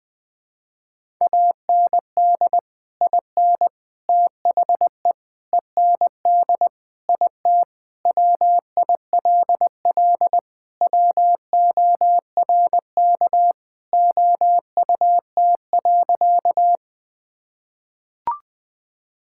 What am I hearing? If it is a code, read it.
Morse code, heard as a keyed tone, about 20 wpm, 700 Hz. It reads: AND IN THE END IT WILL WORK OUT.
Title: Sentences with 3-Word Phrases - Spoken Beginning - 20wpm